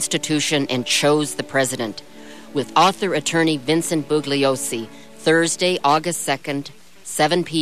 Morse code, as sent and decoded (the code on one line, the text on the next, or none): none